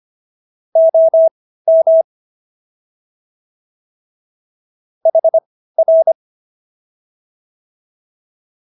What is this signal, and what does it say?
Morse code, heard as a keyed tone, about 25 wpm, spaced as for 15 wpm, 650 Hz. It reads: OM HR